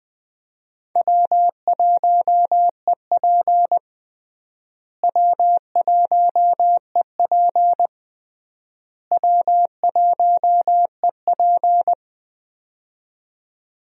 Morse code, heard as a keyed tone, about 20 wpm, 700 Hz. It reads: W1EP W1EP W1EP
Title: USA Call Signs - 4 Characters - Repeated 3x - 20wpm